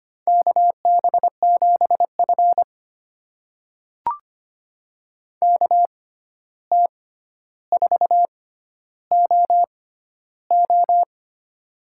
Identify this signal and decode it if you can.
Morse code, heard as a keyed tone, about 25 wpm, spaced as for 12 wpm, 700 Hz. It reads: KB7F KT4OO
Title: USA Call Signs - 6x Character Spacing - 25wpm